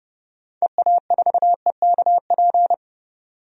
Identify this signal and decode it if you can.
Morse code, heard as a keyed tone, about 32 wpm, 700 Hz. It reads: EA4EKP
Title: DXCC-100 Call Signs - 6 Characters - 30wpm